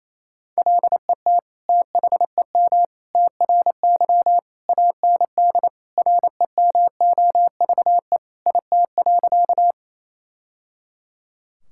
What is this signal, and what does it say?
Morse code, heard as a keyed tone, about 28 wpm, 700 Hz. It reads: LET THEM TRY AND REMOVE IT.
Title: Sentences from Top 1000 Words - Review Part 5 - 28wpm